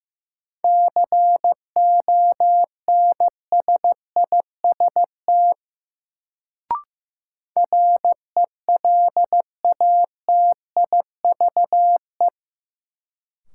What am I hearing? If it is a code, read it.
Morse code, heard as a keyed tone, about 15 wpm, 700 Hz. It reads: CONSIST RELATIVE
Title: Sentences from Top 1400 Words - New Word Drill - 15wpm